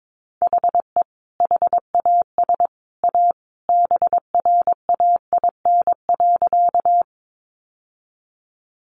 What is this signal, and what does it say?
Morse code, heard as a keyed tone, about 22 wpm, 700 Hz. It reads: HE HAS A BRAIN.
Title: Sentences from Top 1100 Words 22wpm